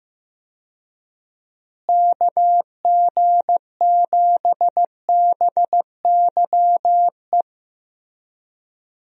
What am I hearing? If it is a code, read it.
Morse code, heard as a keyed tone, about 15 wpm, 700 Hz. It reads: KG7BYE
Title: USA Call Signs - 6 Characters - Repeated 2x - 15wpm